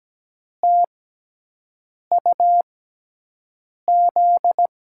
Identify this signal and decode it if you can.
Morse code, heard as a keyed tone, about 17 wpm, spaced as for 6 wpm, 700 Hz. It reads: TUZ
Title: USA Call Signs - 6x Character Spacing - 17wpm